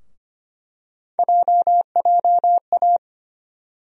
Morse code, heard as a keyed tone, about 25 wpm, 700 Hz. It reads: JJA
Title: USA Call Sign Postfixes - 3 Letters 25wpm